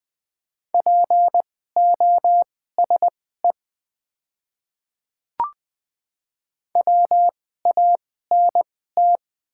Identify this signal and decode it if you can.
Morse code, heard as a keyed tone, about 20 wpm, spaced as for 15 wpm, 690 Hz. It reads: POSE WANT
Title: 2x Character Spacing - Top 500 Words - 20wpm